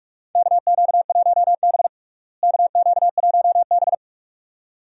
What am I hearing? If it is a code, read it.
Morse code, heard as a keyed tone, about 42 wpm, 700 Hz. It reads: KQ1B KQ1B
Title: USA Call Signs - 4 Characters - Repeated 2x - 45wpm